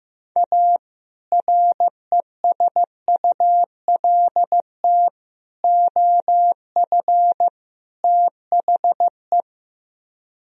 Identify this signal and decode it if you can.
Morse code, heard as a keyed tone, about 15 wpm, 700 Hz. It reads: A RESULT OF THE
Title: Speed-Racing - Sets of 4 Words 15wpm